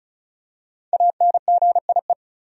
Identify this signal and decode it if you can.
Morse code, heard as a keyed tone, about 36 wpm, 700 Hz. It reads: ANGIE
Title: USA Common Names - Max 5 Letters - 35wpm